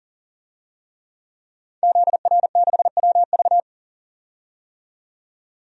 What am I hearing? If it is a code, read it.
Morse code, heard as a keyed tone, about 40 wpm, 700 Hz. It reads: ZR6WV
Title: DXCC-100 Call Signs - 5 Characters - 40wpm